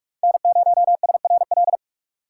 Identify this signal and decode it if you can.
Morse code, heard as a keyed tone, about 46 wpm, 700 Hz. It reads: N0SRL
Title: USA Call Signs - 5 Characters - 3x Character Spacing - 45wpm